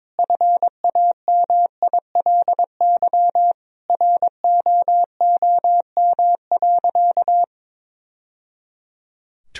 Morse code, heard as a keyed tone, about 22 wpm, 700 Hz. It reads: FAMILY ROOM.
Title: Sentences from Top 1000 Words - Review Part 3 - 22wpm